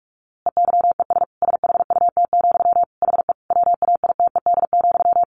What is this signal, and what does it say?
Morse code, heard as a keyed tone, about 45 wpm, 700 Hz. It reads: EYES SHUT, HE WAITED,
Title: A Sound Of Thunder 45wpm